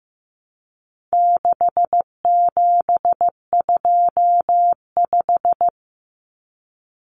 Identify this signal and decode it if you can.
Morse code, heard as a keyed tone, about 15 wpm, 700 Hz. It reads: 6725
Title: Numbers - 4-Digit 15wpm